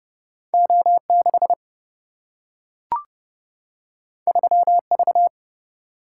Morse code, heard as a keyed tone, about 30 wpm, 700 Hz. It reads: O6 3V
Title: Single-Letter Single-Number 30wpm